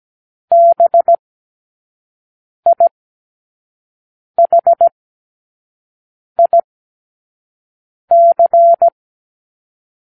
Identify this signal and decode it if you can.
Morse code, heard as a keyed tone, about 17 wpm, spaced as for 5 wpm, 680 Hz. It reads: BIHIC
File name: k7qo / Lesson18(A-L)